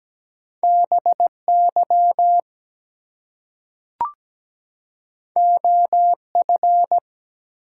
Morse code, heard as a keyed tone, about 17 wpm, 700 Hz. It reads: BY OF
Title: Two Letter Words - 17wpm